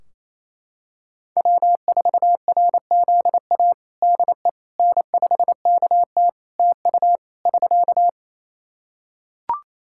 Morse code, heard as a keyed tone, about 27 wpm, 700 Hz. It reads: W4RZA DE N5KT TU <SK>
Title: Practice QSO - Exchange 4 - Ending - 28wpm